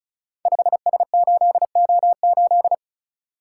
Morse code, heard as a keyed tone, about 36 wpm, 700 Hz. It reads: 5S8O8
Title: Code Groups - 5 Letter:Numbers - 35wpm